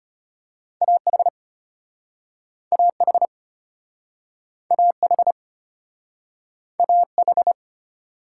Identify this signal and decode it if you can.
Morse code, heard as a keyed tone, about 34 wpm, 700 Hz. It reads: A H A H A H A H